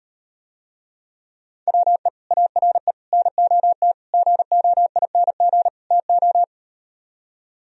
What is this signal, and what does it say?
Morse code, heard as a keyed tone, about 38 wpm, 700 Hz. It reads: WE ARE NOT GOING TO